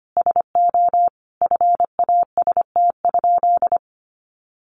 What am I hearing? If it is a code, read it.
Morse code, heard as a keyed tone, about 25 wpm, 700 Hz. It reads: SO FAST?
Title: Sentences from Top 500 Words 25wpm